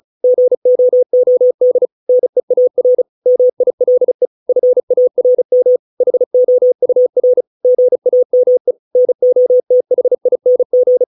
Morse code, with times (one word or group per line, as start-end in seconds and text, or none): 0.2-1.9 GOOD
2.1-3.0 NEAR
3.3-4.3 MILE
4.5-5.8 FARM
6.0-7.4 HOUR
7.6-8.7 GAME
9.0-11.0 NOTHING